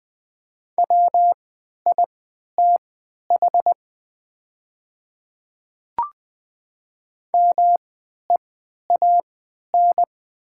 Morse code, heard as a keyed tone, about 20 wpm, spaced as for 11 wpm, 690 Hz. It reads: WITH MEAN
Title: 3x Character Spacing - Top 200 Words - 20wpm